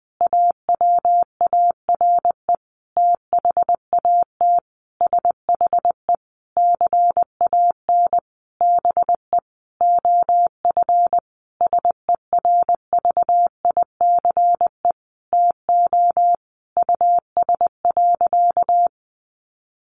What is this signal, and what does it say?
Morse code, heard as a keyed tone, about 20 wpm, 700 Hz. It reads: AWARE THAT SHE CAN BE OF SERVICE TO US.
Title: Sentences from Top 1200 Words 20wpm